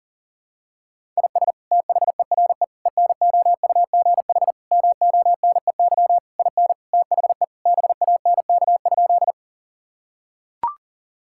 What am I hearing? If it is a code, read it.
Morse code, heard as a keyed tone, about 40 wpm, 700 Hz. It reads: IS THERE ENOUGH MONEY IN THE BANK?
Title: Sentences from Top 1000 Words - Review Part 6 - 40wpm